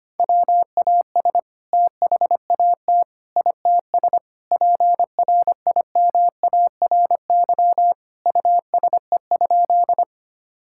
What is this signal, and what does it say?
Morse code, heard as a keyed tone, about 25 wpm, 700 Hz. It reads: WAS THAT ITS PRIMARY USE?
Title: Sentences from Top 1000 Words - Review Part 5 - 25wpm